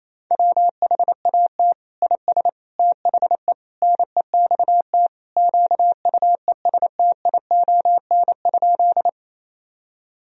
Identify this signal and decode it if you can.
Morse code, heard as a keyed tone, about 28 wpm, 700 Hz. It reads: WHAT IS THE NEXT QUESTION?
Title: Sentences from Top 1000 Words - Review Part 1 - 28wpm